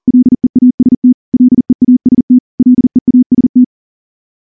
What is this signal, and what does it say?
Morse code, heard as a keyed tone, about 38 wpm, 250 Hz. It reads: LEAST LEAST LEAST
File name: morse / morse-training-40WPM